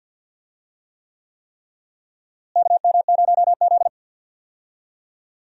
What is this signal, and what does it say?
Morse code, heard as a keyed tone, about 50 wpm, 700 Hz. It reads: KM0Z
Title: USA Call Signs - 4 Characters - Repeated 3x - 50wpm